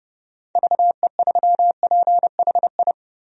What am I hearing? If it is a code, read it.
Morse code, heard as a keyed tone, about 32 wpm, 700 Hz. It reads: VE3PHI
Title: DXCC-100 Call Signs - 6 Characters - 30wpm